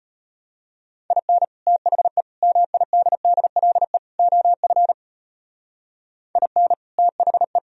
Morse code, heard as a keyed tone, about 32 wpm, 700 Hz. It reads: IN THE MIDDLE OF IN THE